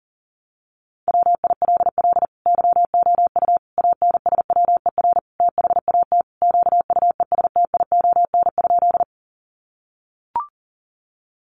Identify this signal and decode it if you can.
Morse code, heard as a keyed tone, about 40 wpm, 700 Hz. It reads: WILL YOU ANSWER THAT QUESTION?
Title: Sentences from Top 400 Words 40wpm